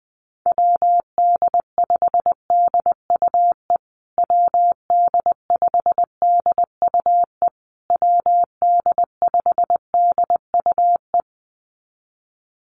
Morse code, heard as a keyed tone, about 20 wpm, 700 Hz. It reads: WD5DUE WD5DUE WD5DUE
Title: USA Call Signs Repeated 3x 20wpm